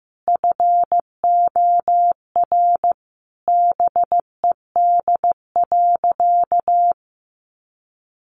Morse code, heard as a keyed tone, about 15 wpm, 700 Hz. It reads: FOR BED.
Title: Sentences from Top 500 Words 15wpm